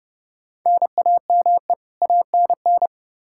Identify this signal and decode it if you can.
Morse code, heard as a keyed tone, about 30 wpm, 700 Hz. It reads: NAME ANN
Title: K1USN SST QSO Practice 30wpm